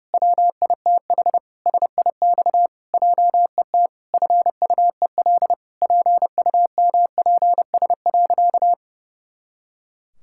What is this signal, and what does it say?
Morse code, heard as a keyed tone, about 30 wpm, 700 Hz. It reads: WITH SIX JET FUEL PUMPS.
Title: Pangram Challenge v2 - 30wpm